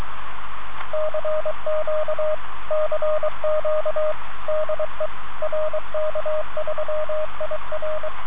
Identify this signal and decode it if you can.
Morse code, heard as a keyed tone, about 23 wpm, 620 Hz, with radio noise. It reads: CQ CQ DE RK3IR